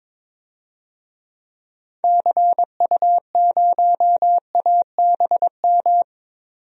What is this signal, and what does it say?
Morse code, heard as a keyed tone, about 22 wpm, 700 Hz. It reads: CU0ABM
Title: DXCC-100 Call Signs - 6 Characters - 22wpm